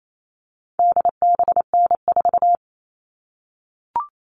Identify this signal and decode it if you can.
Morse code, heard as a keyed tone, about 28 wpm, 700 Hz. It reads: DBN4
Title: All Cut Numbers - 4-Digit 28wpm